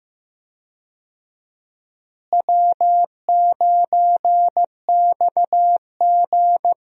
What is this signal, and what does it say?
Morse code, heard as a keyed tone, about 15 wpm, 700 Hz. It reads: W9XG
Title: USA Call Signs - 4 Characters - Repeated 2x - 15wpm